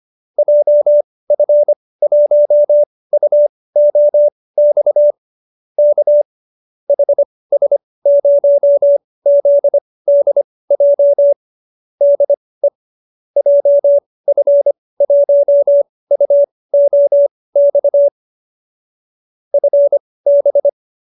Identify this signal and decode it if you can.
Morse code, heard as a keyed tone, about 25 wpm, spaced as for 18 wpm, 570 Hz. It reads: JF1UOX K HS0ZDJ DE JF1UOX FB